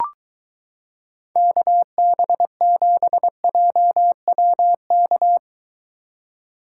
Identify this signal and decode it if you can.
Morse code, heard as a keyed tone, about 23 wpm, 700 Hz. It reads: KB7JWK